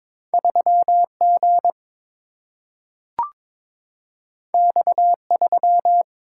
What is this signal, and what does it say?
Morse code, heard as a keyed tone, about 22 wpm, 700 Hz. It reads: 3G X3
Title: Single-Letter Single-Number 22wpm